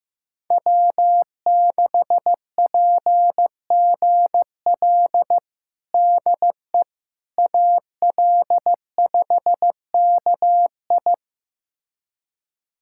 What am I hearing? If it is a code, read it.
Morse code, heard as a keyed tone, about 15 wpm, 700 Hz. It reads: W6PGL DE AL5KI
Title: USA Callsign DE Callsign - 5 Characters - 15wpm